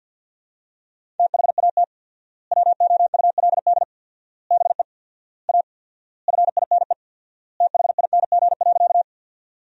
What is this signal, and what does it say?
Morse code, heard as a keyed tone, about 51 wpm, 700 Hz. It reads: THAT WOULD BE A FINE THING.
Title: 4x Word-Spacing - Sentences from Top 500 Words - 50wpm